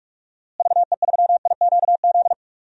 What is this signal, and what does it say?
Morse code, heard as a keyed tone, about 42 wpm, 700 Hz. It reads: VE3IQZ
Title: DXCC-100 Call Signs - 6 Characters - Repeated 3x - 45wpm